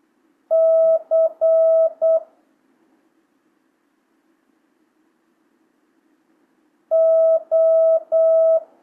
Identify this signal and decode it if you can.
Morse code, heard as a keyed tone, about 8 wpm, 650 Hz. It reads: C O
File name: jh8cbh / 2025cw